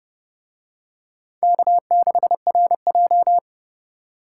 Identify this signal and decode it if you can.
Morse code, heard as a keyed tone, about 29 wpm, 700 Hz. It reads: K6RJ